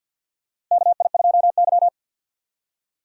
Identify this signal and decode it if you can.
Morse code, heard as a keyed tone, about 53 wpm, 700 Hz. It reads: KI2Y
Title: USA Call Signs - 4 Characters - 3x Character Spacing - 50wpm